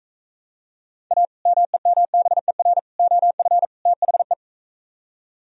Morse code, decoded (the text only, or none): A MEMBER OF THE